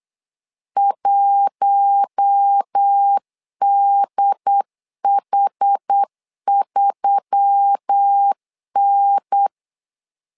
Morse code, heard as a keyed tone, about 8 wpm, 800 Hz. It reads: 1DH3N
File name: LEZIONI / 5 LBVC / 5 LB 8wpm 2816